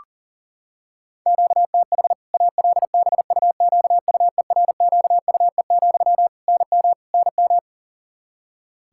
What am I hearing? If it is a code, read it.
Morse code, heard as a keyed tone, about 37 wpm, 700 Hz. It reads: QTH ALBUQUERQUE, NM NM
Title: Practice QSO - Exchange 1 - Beginning - 40wpm